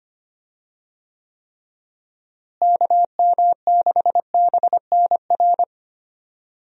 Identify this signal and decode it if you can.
Morse code, heard as a keyed tone, about 25 wpm, 700 Hz. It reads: KM6BNR
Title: USA Call Signs - 6 Characters - Repeated 3x - 25wpm